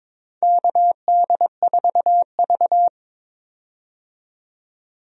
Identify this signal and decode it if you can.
Morse code, heard as a keyed tone, about 22 wpm, 700 Hz. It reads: KD4V
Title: USA Call Signs - 4 Characters - 22wpm